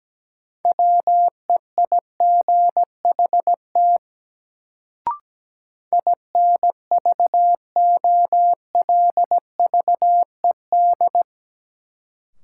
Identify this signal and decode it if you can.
Morse code, heard as a keyed tone, about 17 wpm, 700 Hz. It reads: WEIGHT INVOLVED
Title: Sentences from Top 800 Words - New Word Drill - 17wpm